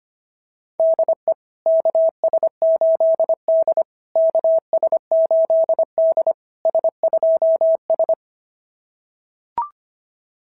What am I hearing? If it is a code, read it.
Morse code, heard as a keyed tone, about 24 wpm, 650 Hz. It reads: DE KS8D KS8D S2S
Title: SOTA QSO Practice 25wpm